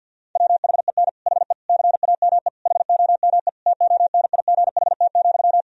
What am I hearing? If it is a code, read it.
Morse code, heard as a keyed tone, about 50 wpm, 700 Hz. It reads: WHEN HE CAME HOME TONIGHT,
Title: Sentences from Top 1000 Words - Review Part 3 - 50wpm